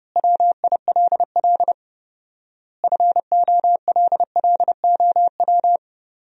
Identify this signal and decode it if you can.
Morse code, heard as a keyed tone, about 31 wpm, 700 Hz. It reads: WILL FOLLOW
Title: 4x Word-Spacing - Sentences from Top 200 Words - 30wpm